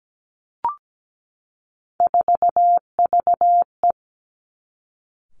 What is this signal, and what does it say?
Morse code, heard as a keyed tone, about 17 wpm, 700 Hz. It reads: 4VE
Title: Three Challenging Letter-Number Combinations 17wpm